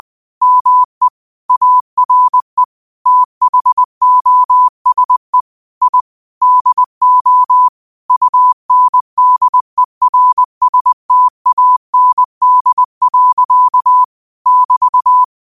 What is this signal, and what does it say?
Morse code, heard as a keyed tone, about 20 wpm, 1000 Hz. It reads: ME ARE THOSE I DO UNDERSTAND. <BT>
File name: qotdfiles / 20wpm